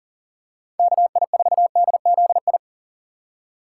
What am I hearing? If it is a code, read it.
Morse code, heard as a keyed tone, about 40 wpm, 700 Hz. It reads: KI4DZI
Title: USA Call Signs - 6 Characters - 2x Character Spacing - 40wpm